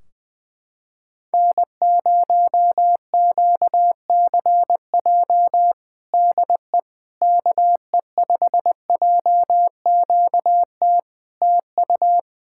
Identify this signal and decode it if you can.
Morse code, heard as a keyed tone, about 20 wpm, 700 Hz. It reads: N0QCJ DE KE5JQT TU
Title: Practice QSO - Exchange 4 - Ending - 20wpm